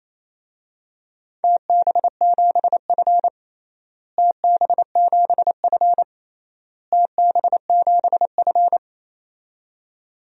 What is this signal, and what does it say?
Morse code, heard as a keyed tone, about 28 wpm, 700 Hz. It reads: TB7F TB7F TB7F
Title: DXCC-100 Call Signs - 4 Characters - Repeated 3x - 28wpm